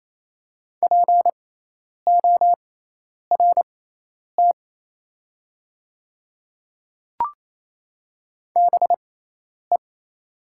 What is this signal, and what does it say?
Morse code, heard as a keyed tone, about 28 wpm, spaced as for 10 wpm, 700 Hz. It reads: PORT BE
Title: 6x Character Spacing - Top 200 Words - 28wpm